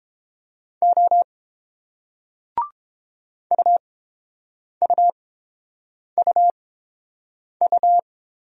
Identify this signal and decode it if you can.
Morse code, heard as a keyed tone, about 30 wpm, spaced as for 7 wpm, 690 Hz. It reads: O UUUU